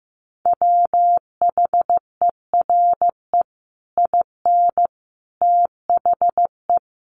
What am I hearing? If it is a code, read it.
Morse code, heard as a keyed tone, about 14 wpm, 700 Hz. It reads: WHERE IN THE
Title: Sentences from Top 200 Words 15wpm